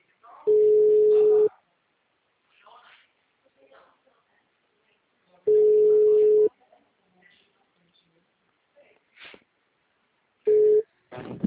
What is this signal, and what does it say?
Ringback tone (Europe, 425 Hz), with speech.